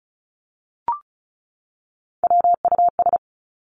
Morse code, heard as a keyed tone, about 36 wpm, 700 Hz. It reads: WUS
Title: Three Challenging Letter-Number Combinations 35wpm